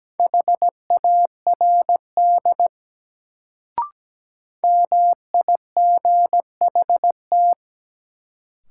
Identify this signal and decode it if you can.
Morse code, heard as a keyed tone, about 17 wpm, 700 Hz. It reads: HARD MIGHT
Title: Sentences from Top 300 Words - New Word Drill - 17wpm